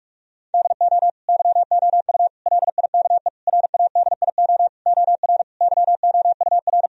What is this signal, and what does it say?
Morse code, heard as a keyed tone, about 41 wpm, 700 Hz. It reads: DO YOU LIKE RADIO OR YOUR
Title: Sentences from Top 1000 Words - Review Part 4 - 45wpm